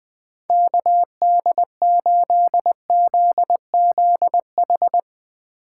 Morse code, heard as a keyed tone, about 20 wpm, 700 Hz. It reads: KD8ZZH
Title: USA Call Signs - 6 Characters - 2x Character Spacing - 20wpm